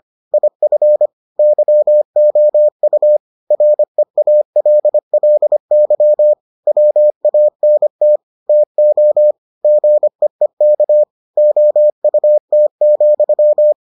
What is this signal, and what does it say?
Morse code, heard as a keyed tone, about 25 wpm, 600 Hz. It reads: IF YOU REALLY WANT TO GEEK OUT,